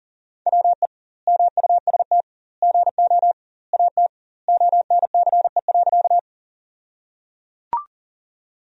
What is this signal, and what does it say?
Morse code, heard as a keyed tone, about 37 wpm, 700 Hz. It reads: WE MUST GO AT ONCE.
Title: 2x Word-Spacing - Sentences from Top 300 Words - 40wpm